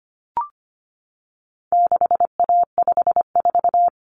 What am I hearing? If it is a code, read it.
Morse code, heard as a keyed tone, about 25 wpm, 700 Hz. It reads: 6A54
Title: Common Cut Numbers - 4-Digit 25wpm